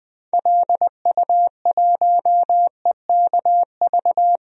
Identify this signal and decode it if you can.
Morse code, heard as a keyed tone, about 20 wpm, 700 Hz. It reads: LU1EKV
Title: DXCC-100 Call Signs - 6 Characters - Repeated 3x - 20wpm